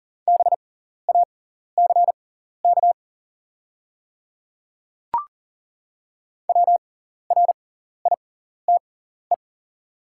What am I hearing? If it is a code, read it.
Morse code, heard as a keyed tone, about 40 wpm, spaced as for 14 wpm, 700 Hz. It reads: BACK WRITE
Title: 6x Character Spacing - Top 300 Words - 40wpm